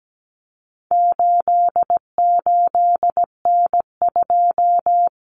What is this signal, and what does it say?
Morse code, heard as a keyed tone, about 17 wpm, 700 Hz. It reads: 88N2
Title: Common Cut Numbers - 4-Digit 17wpm